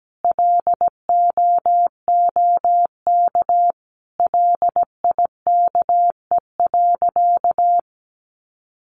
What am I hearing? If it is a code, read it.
Morse code, heard as a keyed tone, about 17 wpm, 700 Hz. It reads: LOOK LIKE.
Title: Sentences from Top 1300 Words 17wpm